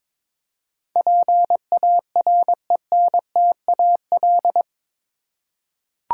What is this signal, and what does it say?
Morse code, heard as a keyed tone, about 21 wpm, 700 Hz. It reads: PARENTAL
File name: practice / 22wpm / Eight Letter Words - 22wpm